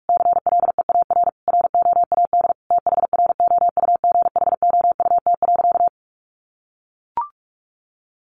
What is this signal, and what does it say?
Morse code, heard as a keyed tone, about 44 wpm, 700 Hz. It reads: CLEAR ROAD THROUGHOUT.
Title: Sentences from Top 800 Words 45wpm